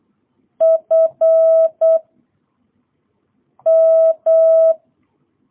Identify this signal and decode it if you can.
Morse code, heard as a keyed tone, about 8 wpm, 660 Hz. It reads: F M